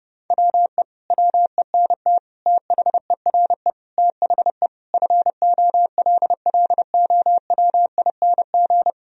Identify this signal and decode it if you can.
Morse code, heard as a keyed tone, about 28 wpm, 700 Hz. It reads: WE WENT THERE THE FOLLOWING
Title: Sentences from Top 1000 Words - Review Part 4 - 30wpm